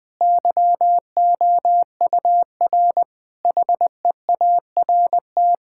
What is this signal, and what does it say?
Morse code, heard as a keyed tone, about 20 wpm, 700 Hz. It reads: YOUR HEART
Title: Sentences with 9-Word Phrases - 20wpm